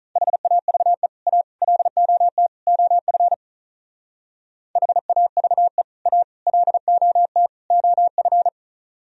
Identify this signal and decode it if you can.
Morse code, heard as a keyed tone, about 44 wpm, 700 Hz. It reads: HAVE A LOT OF HAVE A LOT OF